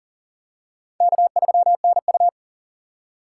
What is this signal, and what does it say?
Morse code, heard as a keyed tone, about 40 wpm, 700 Hz. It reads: K3NU
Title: USA Call Signs - 4 Characters - 3x Character Spacing - 40wpm